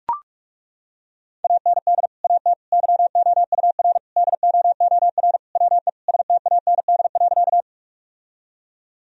Morse code, heard as a keyed tone, about 41 wpm, 700 Hz. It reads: AND AT YOUR DOOR WE STAND.
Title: Sentences from Top 1000 Words - Review Part 1 - 45wpm